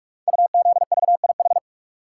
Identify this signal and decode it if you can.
Morse code, heard as a keyed tone, about 44 wpm, 700 Hz. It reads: UZVIH
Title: Code Groups - 5 Letters - 45wpm